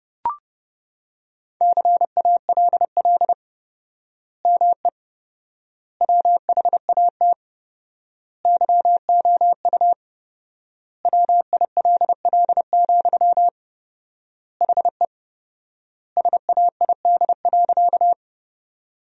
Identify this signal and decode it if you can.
Morse code, heard as a keyed tone, about 32 wpm, 700 Hz. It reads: CALL ME WHAT YOU WILL, HE SAID.
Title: 4x Word-Spacing - Sentences from Top 200 Words - 30wpm